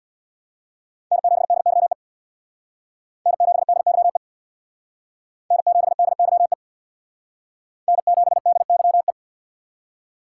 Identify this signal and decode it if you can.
Morse code, heard as a keyed tone, about 64 wpm, 700 Hz. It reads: N7DYE N7DYE N7DYE N7DYE